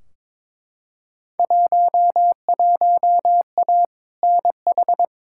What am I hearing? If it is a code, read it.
Morse code, heard as a keyed tone, about 22 wpm, 700 Hz. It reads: 11A NH
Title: Field Day Exchanges Encore - 22wpm